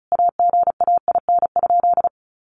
Morse code, heard as a keyed tone, about 34 wpm, 700 Hz. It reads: AGAIN?
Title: Sentences from Top 1300 Words 35wpm